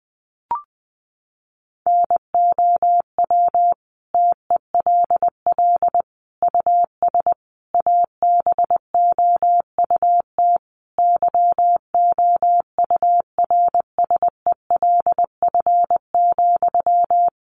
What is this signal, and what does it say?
Morse code, heard as a keyed tone, about 20 wpm, 700 Hz. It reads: NOW TELL US ABOUT YOURSELF,
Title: Sentences from Top 600 Words 20wpm